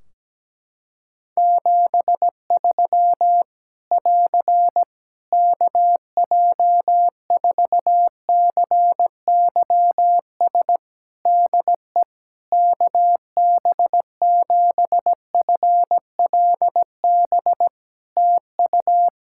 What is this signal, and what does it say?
Morse code, heard as a keyed tone, about 17 wpm, 700 Hz. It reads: 73 <AR> KJ4CYS DE KB7FLB TU
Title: Practice QSO - Exchange 4 - Ending - 17wpm